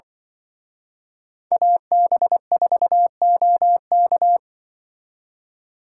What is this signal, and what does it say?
Morse code, heard as a keyed tone, about 24 wpm, 700 Hz. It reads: AB4OK
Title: Speed Racing - USA 2x2 Call Signs 20wpm